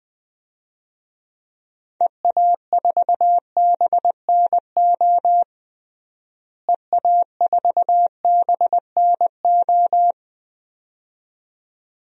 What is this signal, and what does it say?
Morse code, heard as a keyed tone, about 20 wpm, 700 Hz. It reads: EA4BNO EA4BNO
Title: DXCC-100 Call Signs - 6 Characters - Repeated 2x - 20wpm